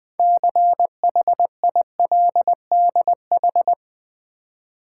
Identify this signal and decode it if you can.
Morse code, heard as a keyed tone, about 20 wpm, 700 Hz. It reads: CHILDH
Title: Build A Word - 3 Letters At A Time - 9-Letter Words - 20wpm